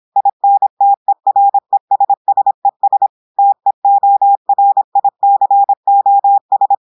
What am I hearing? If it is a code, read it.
Morse code, heard as a keyed tone, about 26 wpm, 800 Hz. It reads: INTERESSES TEORICOS
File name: audios / portugais / portugais texte en clair 1 vitesse 26 mots minutes